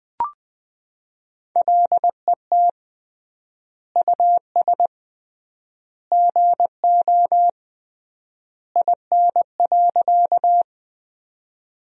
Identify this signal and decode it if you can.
Morse code, heard as a keyed tone, about 20 wpm, 700 Hz. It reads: LET US GO IN.
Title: 3x Word-Spacing - Sentences from Top 300 Words - 20wpm